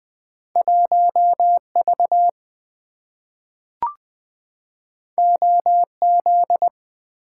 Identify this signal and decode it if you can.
Morse code, heard as a keyed tone, about 20 wpm, 700 Hz. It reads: 1V OZ
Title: Two Letter-Number Combinations - 20wpm